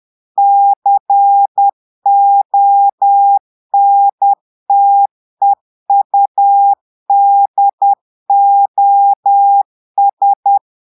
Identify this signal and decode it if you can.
Morse code, heard as a keyed tone, about 10 wpm, 800 Hz. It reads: CONTEUDOS
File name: portugais texte en clair 2 vitesse 10 mots minutes